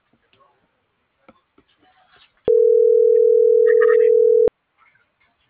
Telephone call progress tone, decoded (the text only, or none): ringback tone